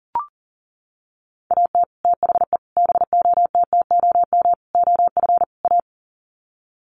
Morse code, heard as a keyed tone, about 40 wpm, 700 Hz. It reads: AT THE BOTTOM OF A
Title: Sets of 5 Words Encore2 40wpm